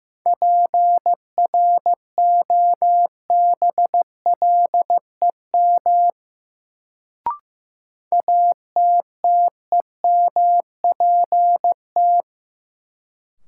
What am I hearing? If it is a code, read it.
Morse code, heard as a keyed tone, about 15 wpm, 700 Hz. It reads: PROBLEM ATTEMPT
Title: Sentences from Top 1000 Words - New Word Drill - 15wpm